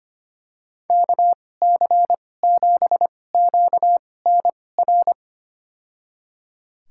Morse code, heard as a keyed tone, about 25 wpm, spaced as for 18 wpm, 700 Hz. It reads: KC7QNR
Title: USA Call Signs - 2x Character Spacing - 25wpm